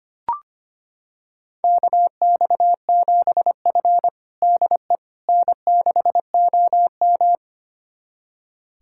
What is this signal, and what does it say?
Morse code, heard as a keyed tone, about 25 wpm, 700 Hz. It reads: KX7F DE N6OM
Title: USA Callsign DE Callsign - 4 Characters - 25wpm